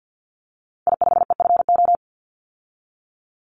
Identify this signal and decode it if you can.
Morse code, heard as a keyed tone, about 50 wpm, 700 Hz. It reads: I5EFO
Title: Call Signs Repeated 2x - 50wpm